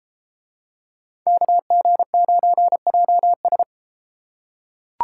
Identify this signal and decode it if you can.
Morse code, heard as a keyed tone, about 31 wpm, 700 Hz. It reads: KG9JS